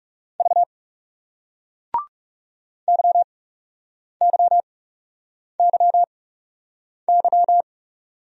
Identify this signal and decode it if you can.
Morse code, heard as a keyed tone, about 39 wpm, spaced as for 8 wpm, 700 Hz. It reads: V YYYY